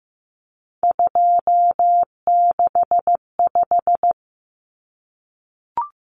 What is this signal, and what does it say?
Morse code, heard as a keyed tone, about 15 wpm, 700 Hz. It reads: 265
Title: Numbers - 3-Digit 15wpm